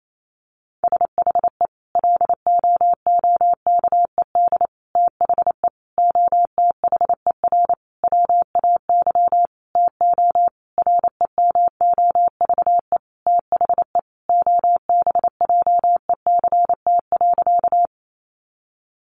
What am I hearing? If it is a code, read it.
Morse code, heard as a keyed tone, about 28 wpm, 700 Hz. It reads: SHE LOOKED THE OTHER WAY TO REMOVE THE OBJECT.
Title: Sentences from Top 800 Words 28wpm